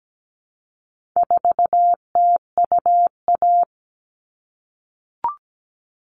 Morse code, heard as a keyed tone, about 17 wpm, 700 Hz. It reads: 4TUA
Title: All Cut Numbers - 4-Digit 17wpm